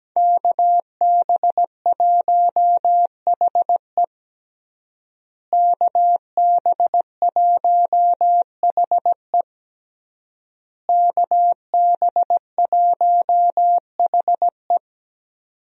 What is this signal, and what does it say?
Morse code, heard as a keyed tone, about 17 wpm, 700 Hz. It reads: KB1HE KB1HE KB1HE
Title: USA Call Signs - 5 Characters - Repeated 3x - 17wpm